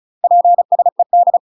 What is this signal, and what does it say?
Morse code, heard as a keyed tone, about 36 wpm, 700 Hz. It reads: PSED
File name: UK MASTER QSOs TRAINING / 35WPM / UK MASTER QSOs 0020